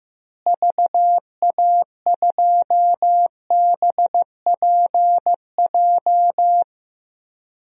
Morse code, heard as a keyed tone, about 15 wpm, 700 Hz. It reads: VA2BPJ
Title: DXCC-100 Call Signs - 6 Characters - Repeated 3x - 15wpm